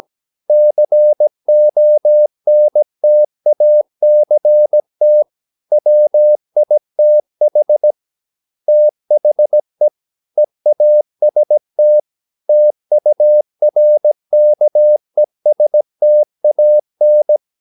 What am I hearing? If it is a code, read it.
Morse code, heard as a keyed tone, about 17 wpm, 600 Hz. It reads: CONTACT WITH THE EAST TURKESTAN
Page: Romanen Never af Ken Follet på engelsk.